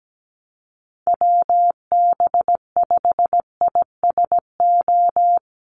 Morse code, heard as a keyed tone, about 17 wpm, 700 Hz. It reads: WB5ISO
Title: USA Call Signs Repeated 3x 17wpm